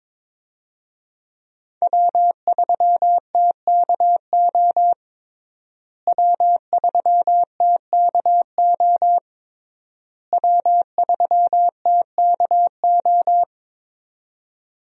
Morse code, heard as a keyed tone, about 22 wpm, 690 Hz. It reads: W3TKO W3TKO W3TKO
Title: USA Call Signs - 5 Characters - Repeated 3x - 22wpm